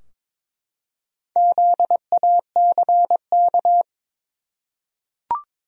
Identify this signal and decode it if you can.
Morse code, heard as a keyed tone, about 22 wpm, 700 Hz. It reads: ZACK
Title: USA Common Names - Max 5 Letters - 22wpm